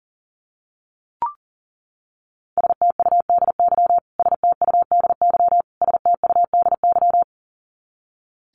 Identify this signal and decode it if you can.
Morse code, heard as a keyed tone, about 40 wpm, 700 Hz. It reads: STUDY STUDY STUDY
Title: Top 500 Words Repeated 40wpm